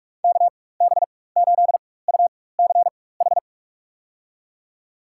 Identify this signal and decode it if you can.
Morse code, heard as a keyed tone, about 46 wpm, spaced as for 21 wpm, 700 Hz. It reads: KB8UCH
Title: USA Call Signs - 4x Character Spacing - 45wpm